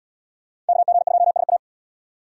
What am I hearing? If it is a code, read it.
Morse code, heard as a keyed tone, about 73 wpm, 700 Hz. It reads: KG2SA